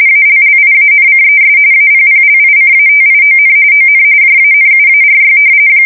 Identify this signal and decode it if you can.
RTTY, about 45.45 baud. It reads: ALLIANCES APPLICATION ON JANUARY
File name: fsk170 / battleshipnj